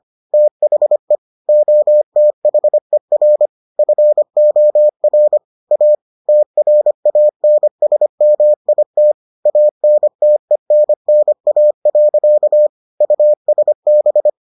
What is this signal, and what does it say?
Morse code, heard as a keyed tone, about 25 wpm, 600 Hz. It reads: THE OTHER FOR A TRANSMIT ANTENNA. USB